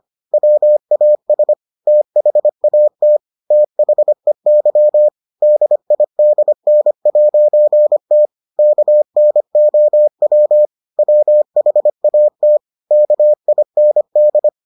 Morse code, heard as a keyed tone, about 25 wpm, 600 Hz. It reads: WAS THAT THEY DIDN'T KNOW WHAT KIND